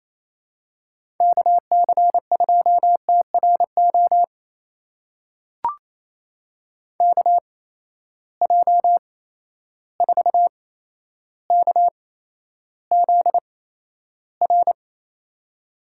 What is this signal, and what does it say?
Morse code, heard as a keyed tone, about 28 wpm, spaced as for 12 wpm, 700 Hz. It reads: KC2TRO KJ4KZR
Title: USA Call Signs - 6 Characters - 8x Character Spacing - 28wpm